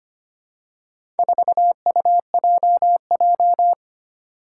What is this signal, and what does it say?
Morse code, heard as a keyed tone, about 25 wpm, 700 Hz. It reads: 4UJJ
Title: Four Challenging Letter-Number Combinations 25wpm